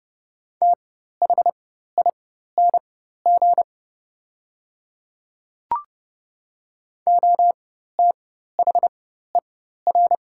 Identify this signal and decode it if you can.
Morse code, heard as a keyed tone, about 30 wpm, spaced as for 14 wpm, 700 Hz. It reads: THING OTHER
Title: 4x Character Spacing - Top 200 Words - 30wpm